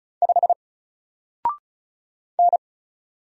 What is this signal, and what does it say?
Morse code, heard as a keyed tone, about 35 wpm, 700 Hz. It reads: 5 N
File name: Single-Letter-Number-35wpm